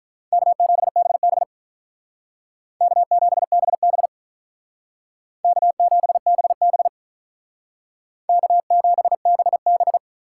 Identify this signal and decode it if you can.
Morse code, heard as a keyed tone, about 46 wpm, 700 Hz. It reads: K7BB K7BB K7BB K7BB